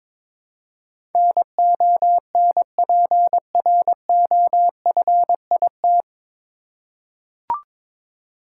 Morse code, heard as a keyed tone, about 22 wpm, 700 Hz. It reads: NONPROFIT
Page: Morse Code Ninja practice 22wpm